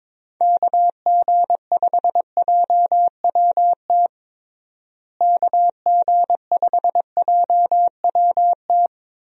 Morse code, heard as a keyed tone, about 22 wpm, 700 Hz. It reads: KG5JWT KG5JWT
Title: USA Call Signs - 6 Characters - Repeated 3x - 22wpm